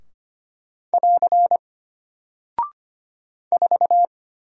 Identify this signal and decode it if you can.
Morse code, heard as a keyed tone, about 25 wpm, 700 Hz. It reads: <AR> 4
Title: Single Character (Letter:Number:Punctuation:Prosign) Alt 25wpm